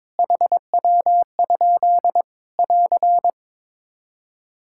Morse code, heard as a keyed tone, about 22 wpm, 700 Hz. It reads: HW? <AR>
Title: Practice QSO - Exchange 2 - Info - 22wpm